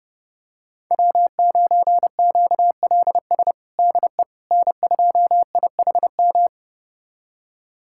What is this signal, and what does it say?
Morse code, heard as a keyed tone, about 30 wpm, 700 Hz. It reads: W9QLS DE N2IHM
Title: USA Callsign DE Callsign - 5 Characters - 30wpm